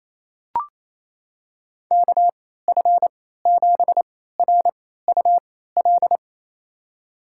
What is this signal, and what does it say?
Morse code, heard as a keyed tone, about 28 wpm, spaced as for 16 wpm, 700 Hz. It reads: KF7RUL
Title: USA Call Signs - 6 Characters - 3x Character Spacing - 28wpm